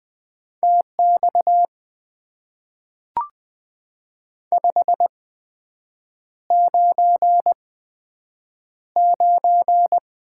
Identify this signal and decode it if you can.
Morse code, heard as a keyed tone, about 20 wpm, 700 Hz. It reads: T X 5 9 9
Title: Top 90 QSO Elements - 8x Character Spacing - 20wpm